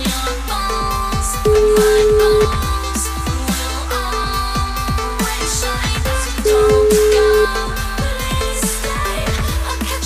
Ringback tone (Europe, 425 Hz), with noise.